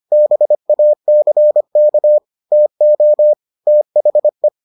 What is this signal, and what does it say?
Morse code, heard as a keyed tone, about 25 wpm, 600 Hz. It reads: BACK TO THE